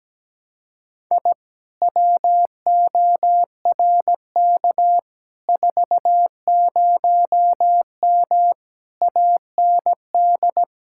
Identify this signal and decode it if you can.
Morse code, heard as a keyed tone, about 17 wpm, 700 Hz. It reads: I WORK 40M AND
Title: Speed-Racing Common QSO Phrases Encore - 17wpm